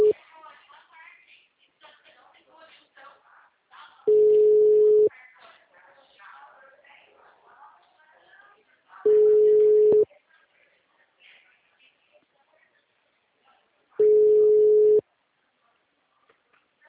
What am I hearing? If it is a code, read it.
Ringback tone (Europe, 425 Hz), with speech.